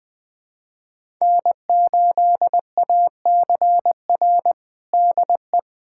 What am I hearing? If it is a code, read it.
Morse code, heard as a keyed tone, about 21 wpm, 700 Hz. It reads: N8ACR DE
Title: USA Callsign DE Callsign 20wpm